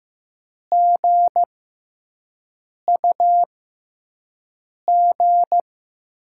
Morse code, heard as a keyed tone, about 15 wpm, spaced as for 5 wpm, 700 Hz. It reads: GUG
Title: USA Call Signs - 6 Characters - 6x Character Spacing - 15wpm